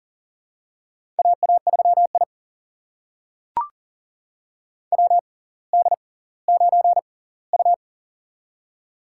Morse code, heard as a keyed tone, about 39 wpm, spaced as for 20 wpm, 690 Hz. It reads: AA3I WD9U
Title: USA Call Signs - 4 Characters - 6x Character Spacing - 40wpm